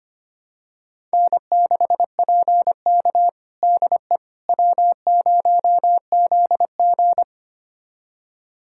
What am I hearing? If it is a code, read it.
Morse code, heard as a keyed tone, about 25 wpm, 700 Hz. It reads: N6PK DE W0ZG
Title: USA Callsign DE Callsign - 4 Characters - 25wpm